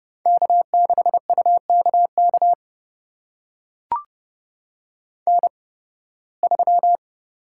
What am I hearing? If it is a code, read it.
Morse code, heard as a keyed tone, about 30 wpm, spaced as for 20 wpm, 700 Hz. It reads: K6UKK N3
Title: USA Call Signs - 5 Characters - 8x Character Spacing - 30wpm